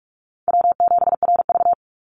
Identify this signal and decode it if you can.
Morse code, heard as a keyed tone, about 46 wpm, 700 Hz. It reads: W7RV
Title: NA SOTA Chaser Callsigns 45wpm